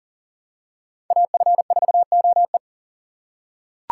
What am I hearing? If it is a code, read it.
Morse code, heard as a keyed tone, about 40 wpm, 700 Hz. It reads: AF4OE
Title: USA Call Signs - 4x Character Spacing - 40wpm